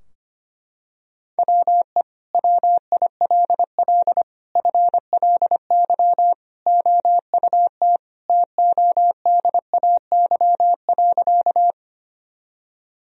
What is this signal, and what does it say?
Morse code, heard as a keyed tone, about 25 wpm, 700 Hz. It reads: WE WILL FLY OUT TODAY.
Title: Sentences from Top 1000 Words - Review Part 2 - 25wpm